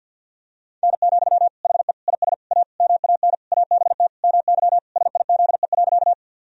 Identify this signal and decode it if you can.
Morse code, heard as a keyed tone, about 52 wpm, 700 Hz. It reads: N, HE IS A MAN ABT MY SIZE.
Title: Sentences from Top 400 Words and Abbreviations 50wpm